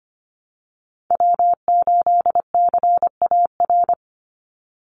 Morse code, heard as a keyed tone, about 25 wpm, 700 Hz. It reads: W8CAR
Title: Call Signs Repeated 3x - 25wpm